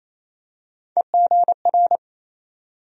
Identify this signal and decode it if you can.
Morse code, heard as a keyed tone, about 28 wpm, 700 Hz. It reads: EGR
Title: USA Call Sign Postfixes - 3 Letters 28wpm